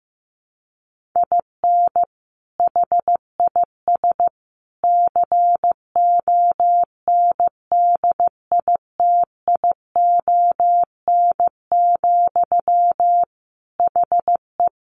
Morse code, heard as a keyed tone, about 15 wpm, 700 Hz. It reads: IN HIS CONDITION, HE
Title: Sentences from Top 1200 Words 15wpm